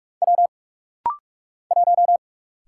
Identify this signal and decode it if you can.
Morse code, heard as a keyed tone, about 44 wpm, 700 Hz. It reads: W 1
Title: Single Letter-Number Alt - Rapid-Fire 45wpm